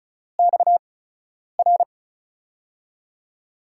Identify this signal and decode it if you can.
Morse code, heard as a keyed tone, about 36 wpm, spaced as for 10 wpm, 700 Hz. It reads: XR
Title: USA Call Signs - 8x Character Spacing - 35wpm